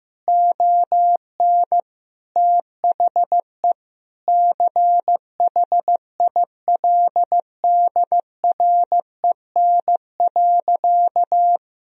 Morse code, heard as a keyed tone, about 15 wpm, 700 Hz. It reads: ON THE CHILDREN.
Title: Sentences from Top 300 Words and Abbreviations 15wpm